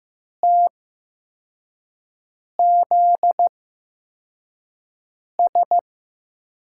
Morse code, heard as a keyed tone, about 15 wpm, spaced as for 4 wpm, 700 Hz. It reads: TZS
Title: USA Call Signs - 5 Characters - 8x Character Spacing - 15wpm